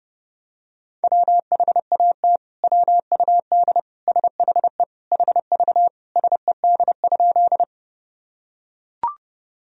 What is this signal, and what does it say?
Morse code, heard as a keyed tone, about 30 wpm, 700 Hz. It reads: WHAT WUD SHE HV SED?
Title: Sentences from Top 100 Words and Abbreviations 30wpm